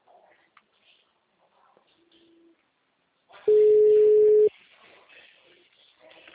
Ringback tone (Europe, 425 Hz).